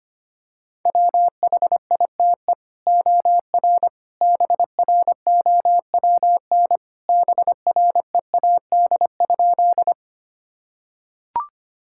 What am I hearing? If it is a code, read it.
Morse code, heard as a keyed tone, about 25 wpm, 700 Hz. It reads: WHITE OR BROWN BREAD?
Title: Sentences from Top 1600 Words 25wpm